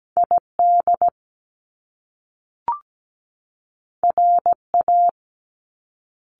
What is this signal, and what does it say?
Morse code, heard as a keyed tone, about 17 wpm, 700 Hz. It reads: ID RA
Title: Top Two Letter Combinations in Words 17wpm